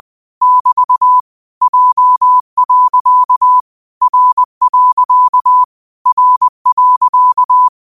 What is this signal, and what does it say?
Morse code, heard as a keyed tone, about 20 wpm, 1000 Hz. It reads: <BT> J. R. R.